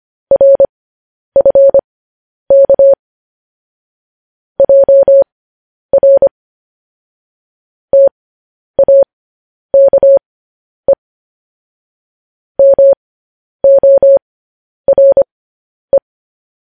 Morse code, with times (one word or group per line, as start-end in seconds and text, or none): 0.3-2.9 RFK
4.6-6.3 JR
7.9-10.9 TAKE
12.6-16.0 MORE